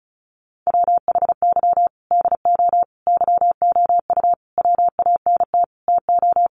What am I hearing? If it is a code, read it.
Morse code, heard as a keyed tone, about 34 wpm, 700 Hz. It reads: WHY DO YOU WANT TO
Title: Sentences from Top 200 Words 35wpm